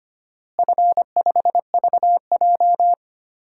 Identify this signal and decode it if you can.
Morse code, heard as a keyed tone, about 25 wpm, 700 Hz. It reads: F5VJ
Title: DXCC-100 Call Signs - 4 Characters - Repeated 2x - 25wpm